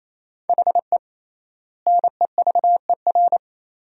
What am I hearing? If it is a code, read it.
Morse code, heard as a keyed tone, about 28 wpm, 700 Hz. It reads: HE NEVER
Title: 3x Word-Spacing - Sentences from Top 400 Words - 28wpm